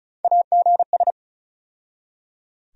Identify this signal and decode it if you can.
Morse code, heard as a keyed tone, about 36 wpm, 700 Hz. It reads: AGS
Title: USA Call Sign Postfixes - 3 Letters 35wpm